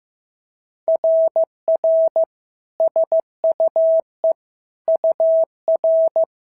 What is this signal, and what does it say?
Morse code, heard as a keyed tone, about 14 wpm, 650 Hz. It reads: RR SUE UR
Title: SOTA QSO Practice 15wpm